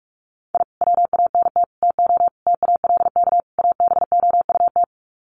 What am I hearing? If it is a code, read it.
Morse code, heard as a keyed tone, about 44 wpm, 700 Hz. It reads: I WANT TO TALK ABOUT
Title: Sets of 5 Words Encore 45wpm